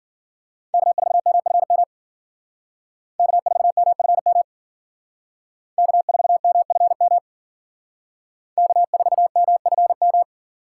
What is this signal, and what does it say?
Morse code, heard as a keyed tone, about 51 wpm, 700 Hz. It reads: K4MFM K4MFM K4MFM K4MFM